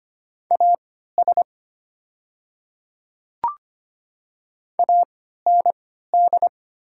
Morse code, heard as a keyed tone, about 25 wpm, spaced as for 14 wpm, 700 Hz. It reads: AS AND
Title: 3x Character Spacing - Top 100 Words - 25wpm